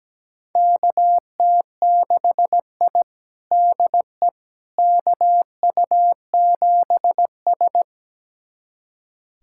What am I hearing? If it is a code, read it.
Morse code, heard as a keyed tone, about 17 wpm, 700 Hz. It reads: KT6I DE KU7S
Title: USA Callsign DE Callsign - 4 Characters - 17wpm